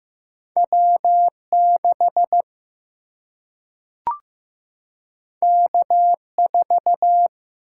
Morse code, heard as a keyed tone, about 15 wpm, 700 Hz. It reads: W 6 K 4
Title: Single-Letter Single-Number 15wpm